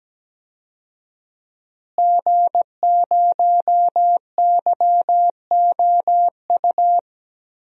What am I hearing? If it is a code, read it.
Morse code, heard as a keyed tone, about 17 wpm, 700 Hz. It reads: G0YOU
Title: DXCC-100 Call Signs - 5 Characters - Repeated 2x - 17wpm